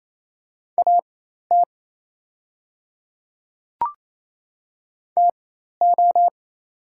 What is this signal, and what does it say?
Morse code, heard as a keyed tone, about 27 wpm, spaced as for 13 wpm, 690 Hz. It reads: AT TO
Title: 4x Character Spacing - Top 200 Words - 28wpm